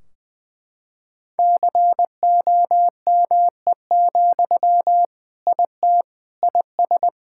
Morse code, heard as a keyed tone, about 20 wpm, 700 Hz. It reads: COME, IT IS
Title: Sentences from Top 1600 Words 20wpm